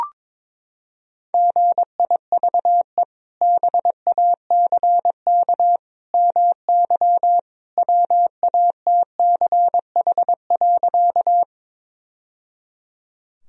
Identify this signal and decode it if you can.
Morse code, heard as a keyed tone, about 22 wpm, 700 Hz. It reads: GIVE BACK MY WATCH.
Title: Sentences from Top 1000 Words - Review Part 1 - 22wpm